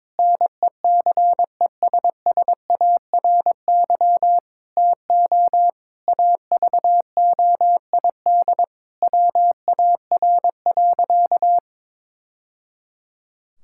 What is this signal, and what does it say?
Morse code, heard as a keyed tone, about 22 wpm, 700 Hz. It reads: NECESSARY TO AVOID WAR.
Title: Sentences from Top 1000 Words - Review Part 6 - 22wpm